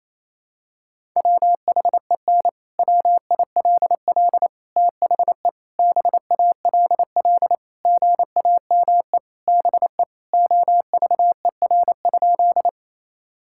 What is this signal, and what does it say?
Morse code, heard as a keyed tone, about 28 wpm, 700 Hz. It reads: WHEN WILL THE BALL GAME BE OVER?
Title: Sentences from Top 1000 Words - Review Part 6 - 28wpm